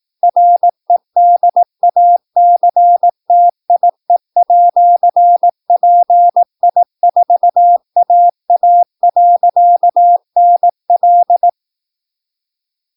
Morse code, heard as a keyed tone, about 18 wpm, 700 Hz. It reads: REDACTIE@PI4AA.NL
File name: PI4AA Uitzending